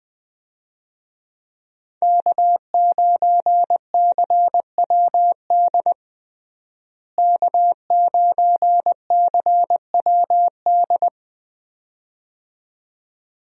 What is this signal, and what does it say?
Morse code, heard as a keyed tone, about 20 wpm, 700 Hz. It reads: K9CWD K9CWD
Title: USA Call Signs - 5 Characters - Repeated 2x - 20wpm